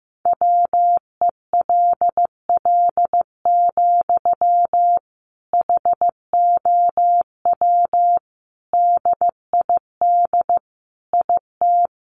Morse code, heard as a keyed tone, about 15 wpm, 700 Hz. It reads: WELL, HOW DID IT
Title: Sentences from Top 200 Words 15wpm